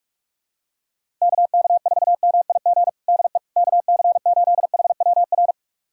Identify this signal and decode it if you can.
Morse code, heard as a keyed tone, about 45 wpm, 700 Hz. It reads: KK4MIG DE KC8HWR
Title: USA Callsign DE Callsign 45wpm